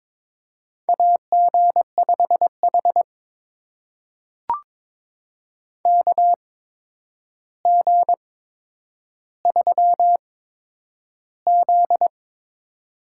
Morse code, heard as a keyed tone, about 22 wpm, spaced as for 9 wpm, 700 Hz. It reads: AG5H KG3Z
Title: USA Call Signs - 4 Characters - 8x Character Spacing - 22wpm